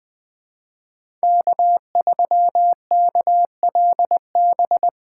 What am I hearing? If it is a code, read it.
Morse code, heard as a keyed tone, about 21 wpm, 700 Hz. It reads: K3KLB
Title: 2023 POTA Activator Call Signs - 20wpm